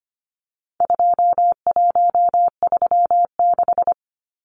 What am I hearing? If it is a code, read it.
Morse code, heard as a keyed tone, about 25 wpm, 700 Hz. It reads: 2136
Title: Numbers - 4-Digit 25wpm